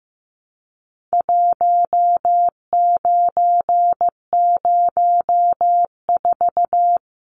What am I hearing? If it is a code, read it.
Morse code, heard as a keyed tone, about 15 wpm, 700 Hz. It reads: 1904
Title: Numbers - 4-Digit 15wpm